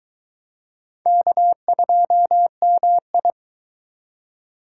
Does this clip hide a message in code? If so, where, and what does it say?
Yes: Morse code from 1.1 to 3.3 s, heard as a keyed tone, about 22 wpm, 700 Hz. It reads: K2MI